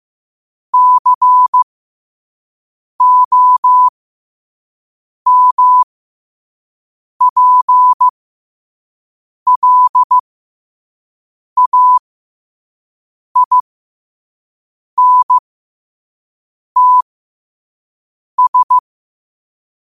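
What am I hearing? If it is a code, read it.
Morse code, heard as a keyed tone, about 15 wpm, spaced as for 5 wpm, 1000 Hz. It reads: COMPLAINTS
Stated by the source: 5 wpm Morse code practice recording.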